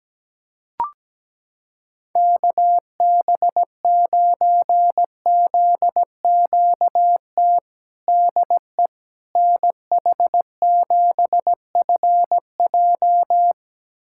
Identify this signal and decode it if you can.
Morse code, heard as a keyed tone, about 17 wpm, 700 Hz. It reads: KB9ZQT DE NH7FJ
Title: USA Callsign DE Callsign 17wpm